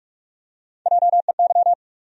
Morse code, heard as a keyed tone, about 46 wpm, 700 Hz. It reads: JEY
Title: USA Call Sign Postfixes - 3 Letters 45wpm